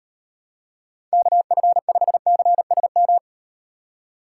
Morse code, heard as a keyed tone, about 38 wpm, 700 Hz. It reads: KF5CSM